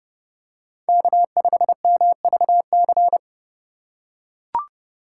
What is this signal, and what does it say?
Morse code, heard as a keyed tone, about 30 wpm, 700 Hz. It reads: K5MVC